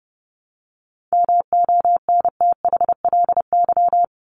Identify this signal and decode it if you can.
Morse code, heard as a keyed tone, about 29 wpm, 700 Hz. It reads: MONTHLY
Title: LY Suffix - Top 100 Words with Base Words 30wpm